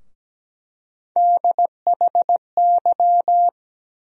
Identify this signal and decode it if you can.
Morse code, heard as a keyed tone, about 17 wpm, 700 Hz. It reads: DHY
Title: USA Call Sign Postfixes - 3 Letters 17wpm